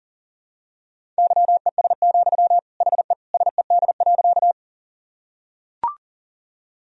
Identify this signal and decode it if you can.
Morse code, heard as a keyed tone, about 40 wpm, 700 Hz. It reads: YES, HE SED.
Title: Sentences from Top 500 Words and Abbreviations 40wpm